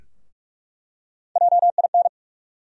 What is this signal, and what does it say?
Morse code, heard as a keyed tone, about 42 wpm, 700 Hz. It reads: JIN
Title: USA Call Sign Postfixes - 3 Letters 45wpm